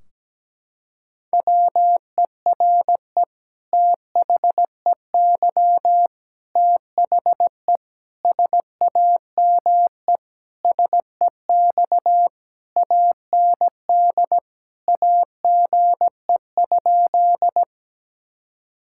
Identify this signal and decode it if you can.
Morse code, heard as a keyed tone, about 17 wpm, 700 Hz. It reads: WERE THEY THE SAME SEX AND AGE?
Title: Sentences from Top 1600 Words 17wpm